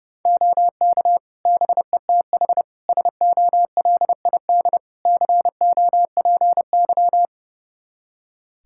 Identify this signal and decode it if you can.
Morse code, heard as a keyed tone, about 30 wpm, 700 Hz. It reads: OK BETH SOLID COPY
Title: Practice QSO - Exchange 3 - More Info - 30wpm